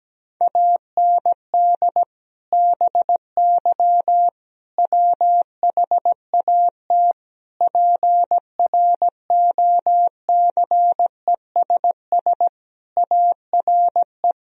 Morse code, heard as a keyed tone, about 17 wpm, 700 Hz. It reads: AND BY WHAT PROCESS ARE
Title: Sentences from Top 1000 Words - Review Part 3 - 17wpm